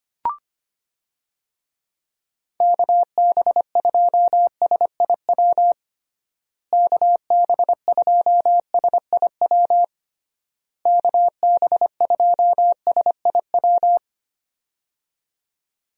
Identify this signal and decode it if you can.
Morse code, heard as a keyed tone, about 25 wpm, 700 Hz. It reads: KB2SIW KB2SIW KB2SIW
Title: USA Call Signs - 6 Characters - Repeated 3x - 25wpm